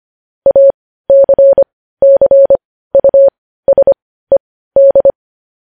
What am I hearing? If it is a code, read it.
Morse code, heard as a keyed tone, about 25 wpm, spaced as for 15 wpm, 550 Hz. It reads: ACCUSED